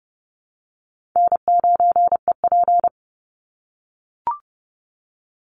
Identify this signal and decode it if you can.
Morse code, heard as a keyed tone, about 30 wpm, 700 Hz. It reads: N9EP
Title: Easy Call Signs - 30wpm